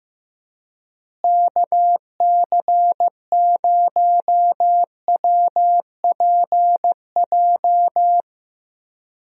Morse code, heard as a keyed tone, about 15 wpm, 700 Hz. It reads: KC0WPJ
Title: USA Call Signs - 6 Characters - Repeated 2x - 15wpm